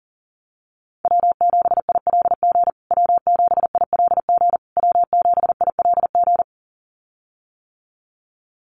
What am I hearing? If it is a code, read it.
Morse code, heard as a keyed tone, about 40 wpm, 700 Hz. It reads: W7ILG W7ILG W7ILG
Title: USA Call Signs Repeated 3x 40wpm